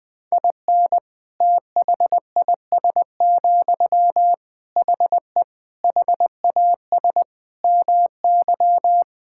Morse code, heard as a keyed tone, about 20 wpm, 700 Hz. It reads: IN THIS, HE HAS MY
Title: Sentences from Top 1000 Words - Review Part 5 - 20wpm